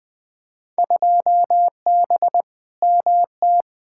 Morse code, heard as a keyed tone, about 20 wpm, 700 Hz. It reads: 2B MT
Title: Field Day Exchanges Encore - 20wpm